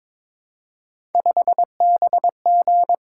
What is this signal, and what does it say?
Morse code, heard as a keyed tone, about 22 wpm, 700 Hz. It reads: 5BG
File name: practice / 22wpm / Three Letter-Number Combinations - 22wpm